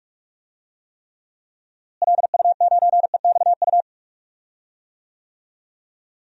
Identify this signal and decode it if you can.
Morse code, heard as a keyed tone, about 46 wpm, 700 Hz. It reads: LU9EXU
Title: DXCC-100 Call Signs - 6 Characters - Repeated 3x - 45wpm